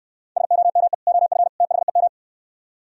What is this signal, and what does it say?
Morse code, heard as a keyed tone, about 69 wpm, 700 Hz. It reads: SOME OF THEM